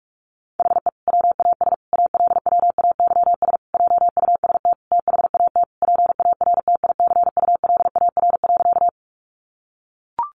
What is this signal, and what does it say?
Morse code, heard as a keyed tone, about 44 wpm, 700 Hz. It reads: HE WAS ALWAYS JUST THAT PARTICULAR.
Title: Sentences from Top 600 Words 45wpm